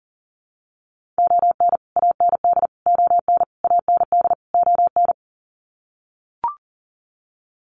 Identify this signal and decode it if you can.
Morse code, heard as a keyed tone, about 40 wpm, 700 Hz. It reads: ON AND ON AND ON
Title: Sets of 5 Words Encore2 40wpm